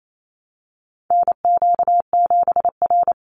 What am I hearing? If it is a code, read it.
Morse code, heard as a keyed tone, about 28 wpm, 700 Hz. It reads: NQ7R
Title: NA SOTA Chaser Callsigns 28wpm